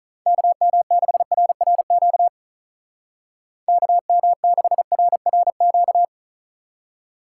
Morse code, heard as a keyed tone, about 45 wpm, 700 Hz. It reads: KM6RRQ KM6RRQ